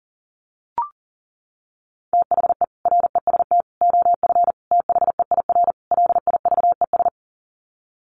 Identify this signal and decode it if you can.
Morse code, heard as a keyed tone, about 40 wpm, 700 Hz. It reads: THE REST OF THEIR LIVES
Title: Sets of 5 Words Encore 40wpm